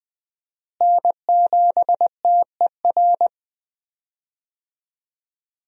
Morse code, heard as a keyed tone, about 20 wpm, 700 Hz. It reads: N7TER
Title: USA Call Signs - 5 Characters - Repeated 3x - 20wpm